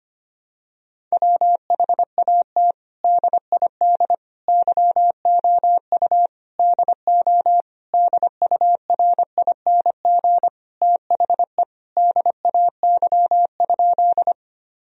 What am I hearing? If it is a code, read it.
Morse code, heard as a keyed tone, about 25 wpm, 700 Hz. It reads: WHAT DID YOU DO DURING THE DAY?
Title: Sentences from Top 1000 Words - Review Part 2 - 25wpm